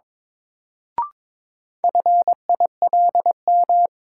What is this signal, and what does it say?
Morse code, heard as a keyed tone, about 22 wpm, 700 Hz. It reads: FILM
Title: Sentences from Top 1000 Words - Review Part 3 - New Word Drill - 22wpm